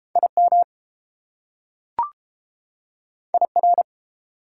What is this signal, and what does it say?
Morse code, heard as a keyed tone, about 32 wpm, 700 Hz. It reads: IM IR